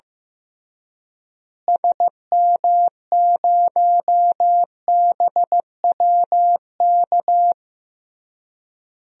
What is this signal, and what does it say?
Morse code, heard as a keyed tone, about 15 wpm, 700 Hz. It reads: SM0BWK
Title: DXCC-100 Call Signs - 6 Characters - Repeated 3x - 15wpm